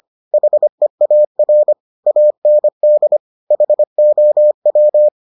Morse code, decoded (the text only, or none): HEAR AND HOW